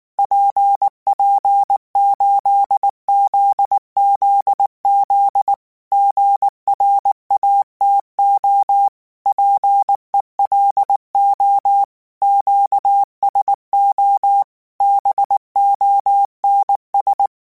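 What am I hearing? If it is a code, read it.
Morse code, heard as a keyed tone, about 19 wpm, 800 Hz. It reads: PP8ZZZ GRATO PELO QSO BONS
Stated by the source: Morse code practice text at 18 wpm, from Portal do CW.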